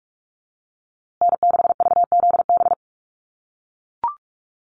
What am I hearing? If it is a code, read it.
Morse code, heard as a keyed tone, about 44 wpm, 700 Hz. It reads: N6VZB
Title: USA Call Signs Repeated 3x 45wpm